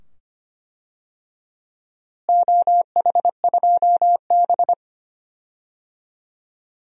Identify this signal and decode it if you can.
Morse code, heard as a keyed tone, about 25 wpm, 700 Hz. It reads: OH2B
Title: DXCC-100 Call Signs - 4 Characters - Repeated 2x - 25wpm